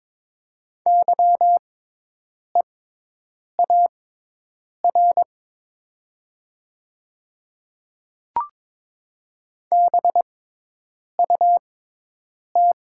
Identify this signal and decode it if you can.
Morse code, heard as a keyed tone, about 22 wpm, spaced as for 8 wpm, 700 Hz. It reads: YEAR BUT
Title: 6x Character Spacing - Top 200 Words - 22wpm